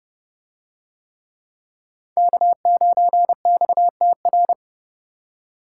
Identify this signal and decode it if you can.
Morse code, heard as a keyed tone, about 27 wpm, 700 Hz. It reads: K9XTR